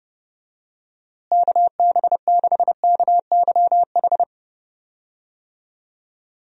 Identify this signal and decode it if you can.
Morse code, heard as a keyed tone, about 32 wpm, 700 Hz. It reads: KB6KYH
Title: USA Call Signs - 6 Characters - Repeated 2x - 30wpm